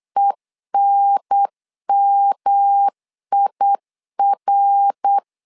Morse code, heard as a keyed tone, about 8 wpm, 800 Hz. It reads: ENMIR
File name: LEZIONI / 4 Q 8wpm 2732